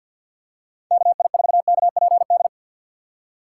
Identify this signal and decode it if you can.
Morse code, heard as a keyed tone, about 53 wpm, 700 Hz. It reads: KI4KPD
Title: USA Call Signs - 6 Characters - 4x Character Spacing - 50wpm